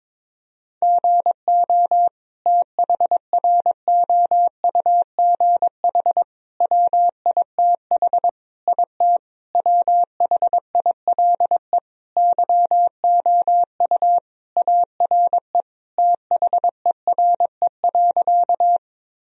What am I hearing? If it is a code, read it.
Morse code, heard as a keyed tone, about 22 wpm, 700 Hz. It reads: GO THROUGH WITH IT WHILE YOU ARE THERE.
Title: Sentences from Top 1000 Words - Review Part 1 - 22wpm